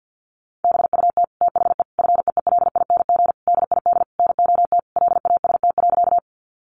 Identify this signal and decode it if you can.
Morse code, heard as a keyed tone, about 50 wpm, 700 Hz. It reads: BUT THE FEELING DID NOT LAST.
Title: Sentences from Top 1200 Words 50wpm